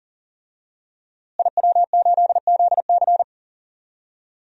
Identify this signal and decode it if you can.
Morse code, heard as a keyed tone, about 40 wpm, 700 Hz. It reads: IW8ZC
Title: DXCC-100 Call Signs - 5 Characters - Repeated 3x - 40wpm